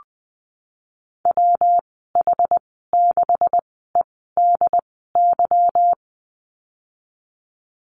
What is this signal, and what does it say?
Morse code, heard as a keyed tone, about 20 wpm, spaced as for 15 wpm, 700 Hz. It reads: WH6EDY
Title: 167 Lesson 27g - Number 7 - Call Signs 20wpm